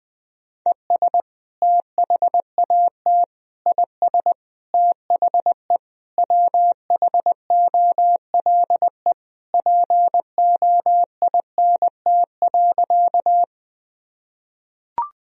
Morse code, heard as a keyed tone, about 21 wpm, 700 Hz. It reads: ES THAT IS THE WHOLE POINT.
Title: Sentences from Top 400 Words and Abbreviations 20wpm